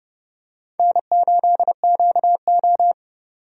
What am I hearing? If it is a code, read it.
Morse code, heard as a keyed tone, about 31 wpm, 700 Hz. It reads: N8QO
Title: USA Call Signs - 4 Characters - 30wpm